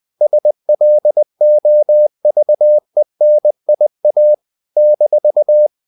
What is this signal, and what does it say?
Morse code, heard as a keyed tone, about 20 wpm, 600 Hz. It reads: SLOVENIA -
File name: CW / Practice / NationalCapitals20